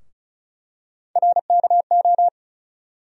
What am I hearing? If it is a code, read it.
Morse code, heard as a keyed tone, about 36 wpm, 700 Hz. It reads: RKO
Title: USA Call Sign Postfixes - 3 Letters 35wpm